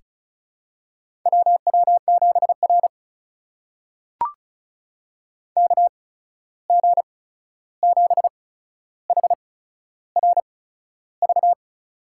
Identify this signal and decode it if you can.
Morse code, heard as a keyed tone, about 36 wpm, spaced as for 13 wpm, 700 Hz. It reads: WW7R KG7HRV
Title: USA Call Signs - 8x Character Spacing - 35wpm